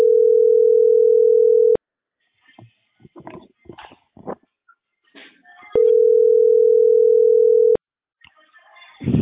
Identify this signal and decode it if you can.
Ringback tone (North America), 440 + 480 Hz.